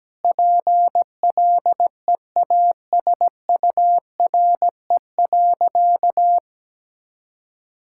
Morse code, heard as a keyed tone, about 17 wpm, 700 Hz. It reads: PLEASURE.
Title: Sentences from Top 1600 Words 17wpm